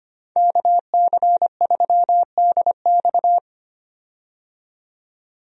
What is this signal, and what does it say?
Morse code, heard as a keyed tone, about 25 wpm, 700 Hz. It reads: KC3DX
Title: USA Call Signs - 5 Characters - 25wpm